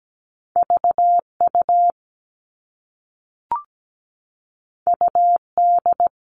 Morse code, heard as a keyed tone, about 17 wpm, 700 Hz. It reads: VU UD
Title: All Cut Numbers - 2-Digit 17wpm